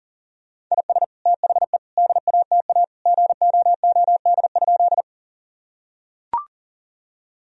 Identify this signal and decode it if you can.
Morse code, heard as a keyed tone, about 40 wpm, 700 Hz. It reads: IS THE DATA GOOD?
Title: Sentences from Top 1000 Words - Review Part 3 - 40wpm